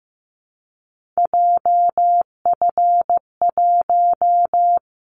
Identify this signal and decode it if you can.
Morse code, heard as a keyed tone, about 15 wpm, 700 Hz. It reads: JF1
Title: Callsign Prefixes for Top 25 DXCC - 15wpm